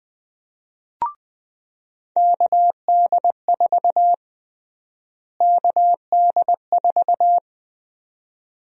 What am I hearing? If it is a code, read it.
Morse code, heard as a keyed tone, about 20 wpm, 700 Hz. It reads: KD4 KD4
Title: USA Call Sign Prefixes - Repeated 2x - 20wpm